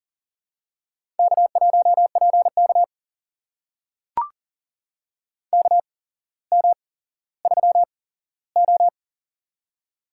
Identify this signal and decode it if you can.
Morse code, heard as a keyed tone, about 40 wpm, spaced as for 17 wpm, 700 Hz. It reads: K1PK KM3O
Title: USA Call Signs - 4 Characters - 8x Character Spacing - 40wpm